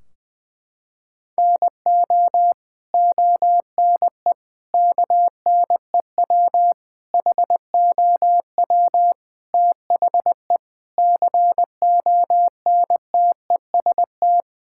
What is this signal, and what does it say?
Morse code, heard as a keyed tone, about 20 wpm, 700 Hz. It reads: NO ONE KNEW HOW THE CONTEST
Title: Sentences from Top 1600 Words 20wpm